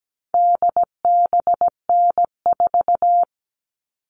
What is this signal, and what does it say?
Morse code, heard as a keyed tone, about 17 wpm, 700 Hz. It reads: DBN4
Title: All Cut Numbers - 4-Digit 17wpm